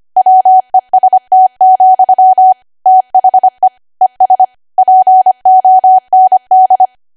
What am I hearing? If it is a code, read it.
Morse code, heard as a keyed tone, about 25 wpm, 750 Hz. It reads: WEST, THE ES POND
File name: cwa / W1AW-Level3 / 25WPM